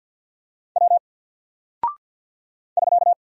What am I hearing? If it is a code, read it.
Morse code, heard as a keyed tone, about 48 wpm, 700 Hz. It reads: W <SK>
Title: Single Character (Letter:Number:Punctuation:Prosign) 50wpm